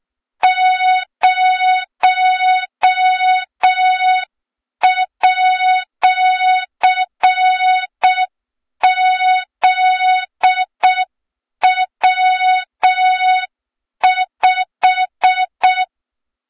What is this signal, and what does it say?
Morse code, heard as a keyed tone, about 6 wpm, 3700 Hz. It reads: 0@ZW5
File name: Morze-HA2MN / hang / 23 GYAKORLAT